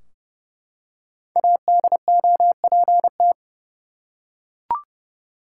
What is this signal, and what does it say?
Morse code, heard as a keyed tone, about 31 wpm, 700 Hz. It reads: ADOPT
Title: Five Letter Words - 30wpm